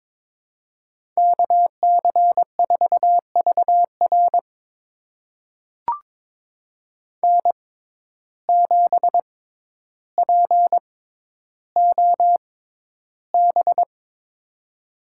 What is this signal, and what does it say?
Morse code, heard as a keyed tone, about 22 wpm, spaced as for 11 wpm, 700 Hz. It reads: KC4VR N7POB
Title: USA Call Signs - 6x Character Spacing - 22wpm